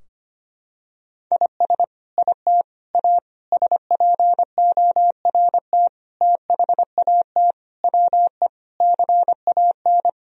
Morse code, heard as a keyed tone, about 25 wpm, 700 Hz. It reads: IS IT A SPORT THAT WE CAN
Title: Sentences from Top 1000 Words - Review Part 4 - 25wpm